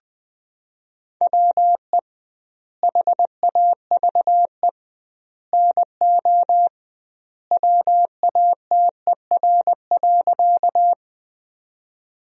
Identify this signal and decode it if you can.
Morse code, heard as a keyed tone, about 21 wpm, 700 Hz. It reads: WE HAVE NO WATER.
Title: 2x Word-Spacing - Sentences from Top 100 Words - 20wpm